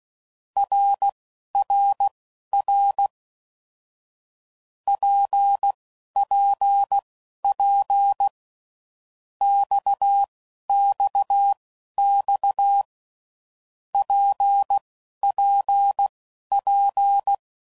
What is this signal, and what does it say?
Morse code, heard as a keyed tone, about 16 wpm, spaced as for 11 wpm, 800 Hz. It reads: RRR PPP XXX PPP
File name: Groups / Basic Letter Groups / RKPX